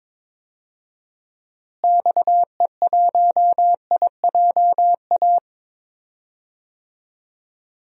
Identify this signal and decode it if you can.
Morse code, heard as a keyed tone, about 22 wpm, 700 Hz. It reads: XE1IJA
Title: DXCC-100 Call Signs - 6 Characters - Repeated 2x - 22wpm